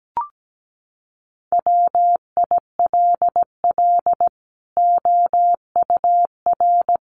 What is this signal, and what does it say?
Morse code, heard as a keyed tone, about 17 wpm, 700 Hz. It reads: WILL OUR
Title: Sentences from Top 1300 Words 17wpm